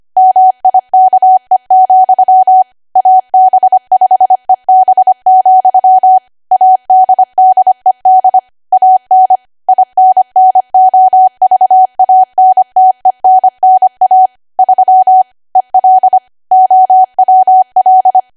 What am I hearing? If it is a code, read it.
Morse code, heard as a keyed tone, about 25 wpm, 750 Hz. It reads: MIKE, AB5EB, ADDED AN INNOVANTENNA 3 EL OWL
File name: cwa / W1AW-Level3 / 25WPM